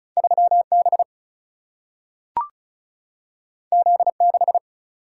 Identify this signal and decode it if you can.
Morse code, heard as a keyed tone, about 36 wpm, 700 Hz. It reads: 3 B Z 6
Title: Single-Letter Single-Number 35wpm